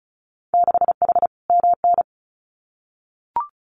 Morse code, heard as a keyed tone, about 36 wpm, 700 Hz. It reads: 6H MN
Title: Winter Field Day Exchanges Encore 35wpm